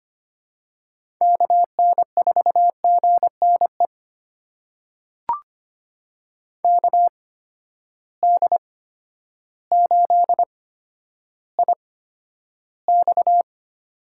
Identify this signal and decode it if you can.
Morse code, heard as a keyed tone, about 25 wpm, spaced as for 11 wpm, 700 Hz. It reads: KN4GNE KD8IX
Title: USA Call Signs - 8x Character Spacing - 25wpm